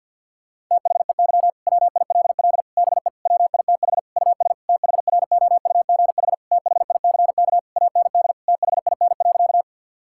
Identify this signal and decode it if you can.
Morse code, heard as a keyed tone, about 50 wpm, 700 Hz. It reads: THEY WILL BE WITH US THROUGH THICK AND THIN.
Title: Sentences from Top 1600 Words 50wpm